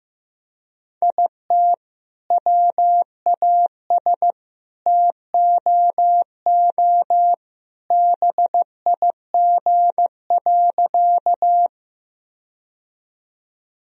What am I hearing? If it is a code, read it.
Morse code, heard as a keyed tone, about 16 wpm, 700 Hz. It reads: IT WAS TOO BIG.
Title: Sentences from Top 1000 Words - Review Part 1 - 15wpm